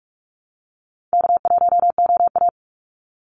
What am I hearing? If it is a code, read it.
Morse code, heard as a keyed tone, about 44 wpm, 700 Hz. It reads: K1OA
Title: Call Signs Repeated 2x - 45wpm